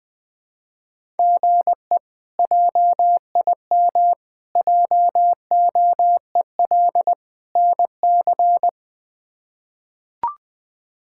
Morse code, heard as a keyed tone, about 21 wpm, 700 Hz. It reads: GE JIM JOEL NC
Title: K1USN SST QSO Practice - 20wpm